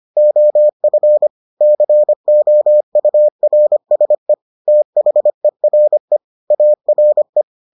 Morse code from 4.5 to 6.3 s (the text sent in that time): THERE